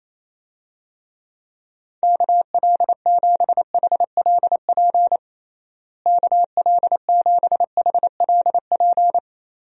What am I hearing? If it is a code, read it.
Morse code, heard as a keyed tone, about 27 wpm, 700 Hz. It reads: KL7HLP KL7HLP
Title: DXCC-100 Call Signs - 6 Characters - Repeated 3x - 28wpm